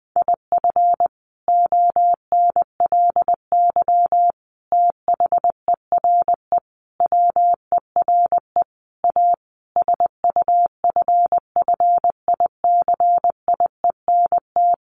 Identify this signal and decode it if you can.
Morse code, heard as a keyed tone, about 20 wpm, 700 Hz. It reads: IF ONLY THERE WERE A SUFFICIENT
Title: Sentences from Top 1400 Words 20wpm